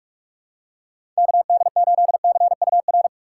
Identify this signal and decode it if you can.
Morse code, heard as a keyed tone, about 46 wpm, 700 Hz. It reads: KD8CUR
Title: USA Call Signs - 6 Characters - 3x Character Spacing - 45wpm